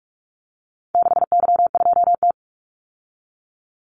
Morse code, heard as a keyed tone, about 44 wpm, 700 Hz. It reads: 6Y2T
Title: Call Signs Repeated 3x - 45wpm